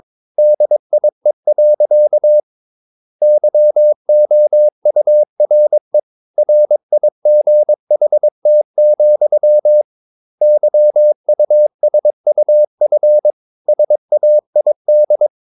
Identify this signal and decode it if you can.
Morse code, heard as a keyed tone, about 22 wpm, 600 Hz. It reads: DIE. YOURE RIGHT, YUSUF SAID
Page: Romanen Never af Ken Follet på engelsk.